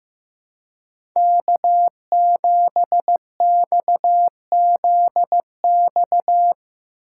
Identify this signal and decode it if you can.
Morse code, heard as a keyed tone, about 15 wpm, 700 Hz. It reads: K7XZX
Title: USA Call Signs - 5 Characters - 2x Character Spacing - 15wpm